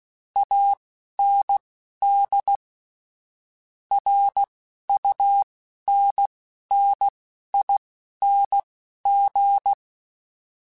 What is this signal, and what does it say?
Morse code, heard as a keyed tone, about 16 wpm, spaced as for 11 wpm, 800 Hz. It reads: AND RUNNING
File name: (spaced) / QSOs Spaced-mp3s / WD0FBT 1 @15 sp